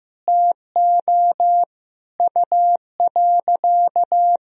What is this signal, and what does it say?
Morse code, heard as a keyed tone, about 15 wpm, 700 Hz. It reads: TO U.
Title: Sentences from Top 500 Words and Abbreviations 15wpm